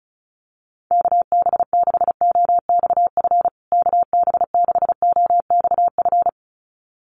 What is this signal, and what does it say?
Morse code, heard as a keyed tone, about 34 wpm, 700 Hz. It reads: KB6OXF KB6OXF
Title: USA Call Signs Repeated 2x 35wpm